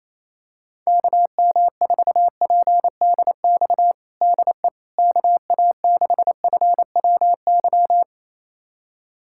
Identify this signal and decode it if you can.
Morse code, heard as a keyed tone, about 28 wpm, 700 Hz. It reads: KM4PDX DE KA6FWY
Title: USA Callsign DE Callsign - 6 Characters - 28wpm